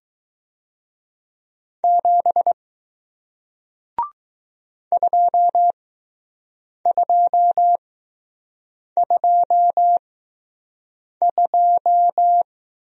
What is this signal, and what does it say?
Morse code, heard as a keyed tone, about 20 wpm, spaced as for 6 wpm, 700 Hz. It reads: 7 2222